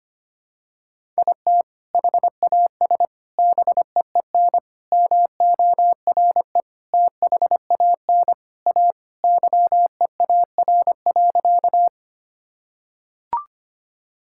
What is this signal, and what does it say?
Morse code, heard as a keyed tone, about 25 wpm, 700 Hz. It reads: IT HAS BEEN MORE THAN A YEAR.
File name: practice / 25wpm / Sentences with 4-Word Phrases - 25wpm